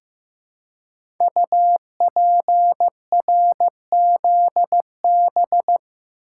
Morse code, heard as a keyed tone, about 15 wpm, 700 Hz. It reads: UPRZB
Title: Code Groups - 5 Letters - 15wpm